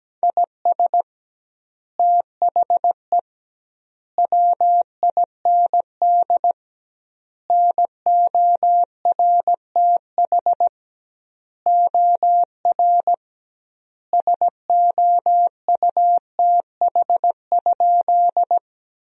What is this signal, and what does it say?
Morse code, heard as a keyed tone, about 17 wpm, 700 Hz. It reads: IS THE WIND NORTH OR SOUTH?
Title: 2x Word-Spacing - Sentences from Top 400 Words - 17wpm